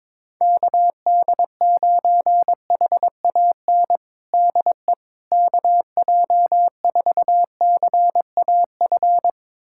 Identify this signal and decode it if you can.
Morse code, heard as a keyed tone, about 22 wpm, 700 Hz. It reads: KD9HAN DE KJ4CAF
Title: USA Callsign DE Callsign 22wpm